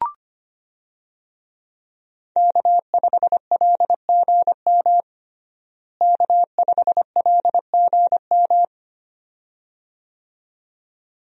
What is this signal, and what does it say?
Morse code, heard as a keyed tone, about 25 wpm, 700 Hz. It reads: K5LGM K5LGM
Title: USA Call Signs - 5 Characters - Repeated 2x - 25wpm